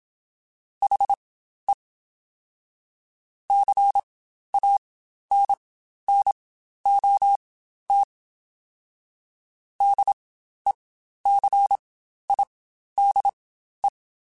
Morse code, heard as a keyed tone, about 27 wpm, spaced as for 12 wpm, 800 Hz. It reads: HE CANNOT DECIDE
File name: Spaced / Chuckles Spaced-mp3s / HalfFull sp @25